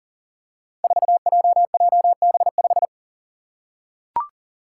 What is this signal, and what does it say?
Morse code, heard as a keyed tone, about 42 wpm, 700 Hz. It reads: 4JJB5
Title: Code Groups - 5 Letter:Numbers - 40wpm